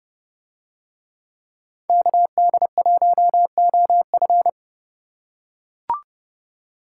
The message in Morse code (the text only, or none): KD1OF